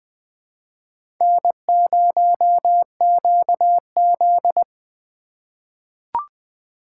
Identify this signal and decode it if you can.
Morse code, heard as a keyed tone, about 20 wpm, 700 Hz. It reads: N0QZ
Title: USA Call Signs - 4 Characters - 4x Character Spacing - 20wpm